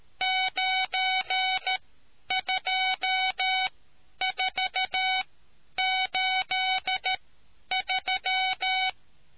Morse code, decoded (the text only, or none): 92483